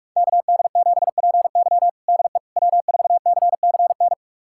Morse code, heard as a keyed tone, about 46 wpm, 700 Hz. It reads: KD7PY DE W4CCN
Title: USA Callsign DE Callsign - 5 Characters - 45wpm